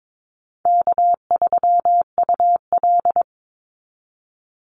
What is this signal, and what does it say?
Morse code, heard as a keyed tone, about 22 wpm, 700 Hz. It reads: K3UL
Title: Call Signs Repeated 2x - 22wpm